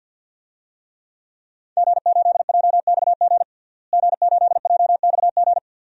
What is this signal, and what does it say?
Morse code, heard as a keyed tone, about 50 wpm, 700 Hz. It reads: G8JXG G8JXG
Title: DXCC-100 Call Signs - 5 Characters - Repeated 2x - 50wpm